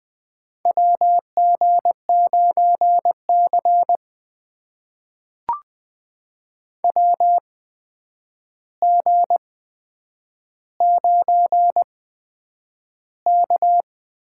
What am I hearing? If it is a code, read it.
Morse code, heard as a keyed tone, about 20 wpm, spaced as for 9 wpm, 700 Hz. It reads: WG9C WG9K
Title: USA Call Signs - 4 Characters - 8x Character Spacing - 20wpm